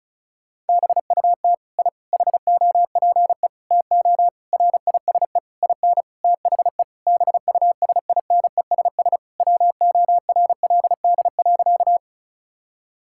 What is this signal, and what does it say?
Morse code, heard as a keyed tone, about 34 wpm, 700 Hz. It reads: BUT I HOPE TO RISE IN THE BUSINESS WORLD.
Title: Sentences from Top 1000 Words - Review Part 6 - 35wpm